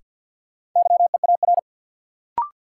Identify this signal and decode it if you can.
Morse code, heard as a keyed tone, about 50 wpm, 700 Hz. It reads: YEAR
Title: Sets of 2 Words - New Word Drill - 50wpm